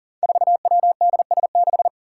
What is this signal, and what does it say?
Morse code, heard as a keyed tone, about 40 wpm, 700 Hz. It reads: 4WDS6
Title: Five Challenging Letter-Number Combinations 40wpm